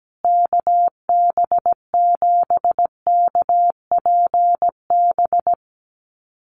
Morse code, heard as a keyed tone, about 17 wpm, 700 Hz. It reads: KB7KPB
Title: USA Call Signs Repeated 3x 17wpm